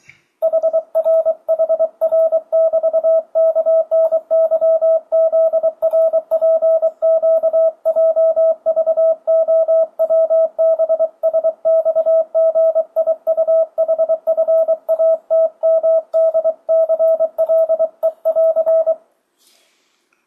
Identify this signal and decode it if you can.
Morse code, heard as a keyed tone, about 23 wpm, 650 Hz. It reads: HRHR<BT>KNYZRPQJVOWBSXGIUHFATMDCLE<AR>
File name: 2025cw / speedup / 24wpm